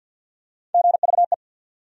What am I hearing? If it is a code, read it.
Morse code, heard as a keyed tone, about 49 wpm, 700 Hz. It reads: GVE
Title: USA Call Sign Postfixes - 3 Letters 50wpm